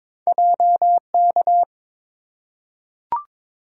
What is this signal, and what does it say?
Morse code, heard as a keyed tone, about 23 wpm, 700 Hz. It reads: JK
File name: practice / 22wpm / Two Letter-Number Combinations - 22wpm